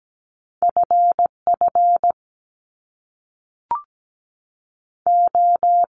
Morse code, heard as a keyed tone, about 17 wpm, 700 Hz. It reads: FF O
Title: Top Two Letter Combinations in Words 17wpm